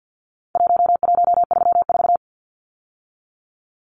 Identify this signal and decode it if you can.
Morse code, heard as a keyed tone, about 50 wpm, 700 Hz. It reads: 1134
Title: Numbers - Contest Style - 4-Digit 50wpm